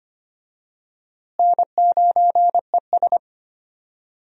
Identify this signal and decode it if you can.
Morse code, heard as a keyed tone, about 25 wpm, 700 Hz. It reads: N9ES
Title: USA Call Signs - 4 Characters - Repeated 3x - 25wpm